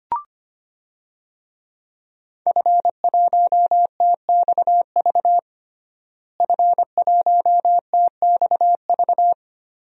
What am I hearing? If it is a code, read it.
Morse code, heard as a keyed tone, about 25 wpm, 700 Hz. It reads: F1TXV F1TXV
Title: DXCC-100 Call Signs - 5 Characters - Repeated 2x - 25wpm